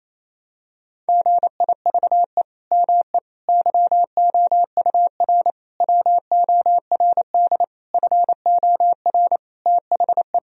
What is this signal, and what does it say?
Morse code, heard as a keyed tone, about 28 wpm, 700 Hz. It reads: GIVE ME YOUR WORD FOR THE
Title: Sentences from Top 1000 Words - Review Part 3 - 28wpm